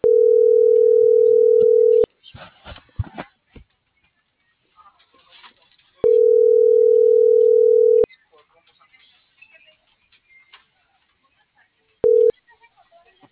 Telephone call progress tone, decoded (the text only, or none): ringback tone